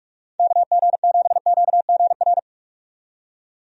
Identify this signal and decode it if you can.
Morse code, heard as a keyed tone, about 48 wpm, 700 Hz. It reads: KG7QGR
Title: USA Call Signs - 6 Characters - 45wpm